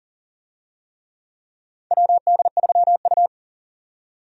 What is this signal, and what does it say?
Morse code, heard as a keyed tone, about 40 wpm, 700 Hz. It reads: WD3U
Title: USA Call Signs - 4 Characters - 40wpm